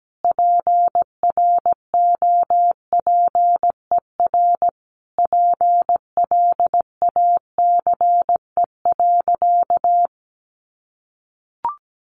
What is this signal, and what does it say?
Morse code, heard as a keyed tone, about 17 wpm, 700 Hz. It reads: PROPER PLACE.
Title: Sentences from Top 800 Words 17wpm